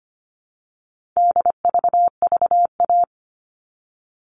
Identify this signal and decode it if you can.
Morse code, heard as a keyed tone, about 25 wpm, 700 Hz. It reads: DVVA
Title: All Cut Numbers - 4-Digit 25wpm